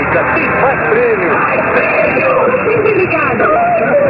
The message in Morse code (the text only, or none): none